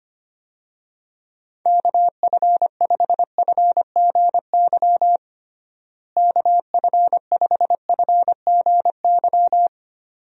Morse code, heard as a keyed tone, about 25 wpm, 700 Hz. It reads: KF5FGY KF5FGY
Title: USA Call Signs - 6 Characters - Repeated 2x - 25wpm